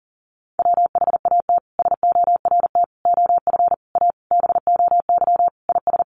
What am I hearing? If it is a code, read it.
Morse code, heard as a keyed tone, about 40 wpm, 700 Hz. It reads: WHAT SORT OF A BOY IS
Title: Sentences from Top 1300 Words 40wpm